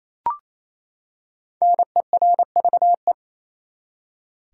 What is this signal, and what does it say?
Morse code, heard as a keyed tone, about 28 wpm, 700 Hz. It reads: NERVE
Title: Five Letter Words - 28wpm